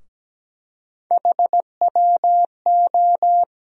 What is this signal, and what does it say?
Morse code, heard as a keyed tone, about 17 wpm, 700 Hz. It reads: HWO
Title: USA Call Sign Postfixes - 3 Letters 17wpm